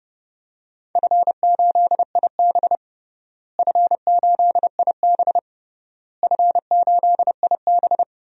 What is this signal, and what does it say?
Morse code, heard as a keyed tone, about 30 wpm, 700 Hz. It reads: F8IB F8IB F8IB
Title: DXCC-100 Call Signs - 4 Characters - Repeated 3x - 30wpm